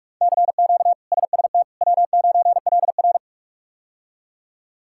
Morse code, heard as a keyed tone, about 46 wpm, 700 Hz. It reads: CQ SST W9LR
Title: K1USN SST QSO Practice - 45wpm